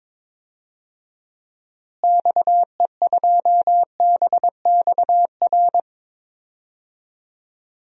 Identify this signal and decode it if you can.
Morse code, heard as a keyed tone, about 21 wpm, 700 Hz. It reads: XE2BXR
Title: DXCC-100 Call Signs - 6 Characters - Repeated 3x - 22wpm